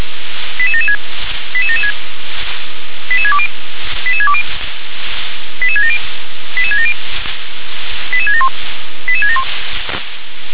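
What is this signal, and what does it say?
Five-tone selective call (ZVEI), heard as quick runs of five tones, tones of about 70 ms. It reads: 88776 88776 88620 88620 88670 88670 88671 88671
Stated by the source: ZVEI five-tone call received on SDR.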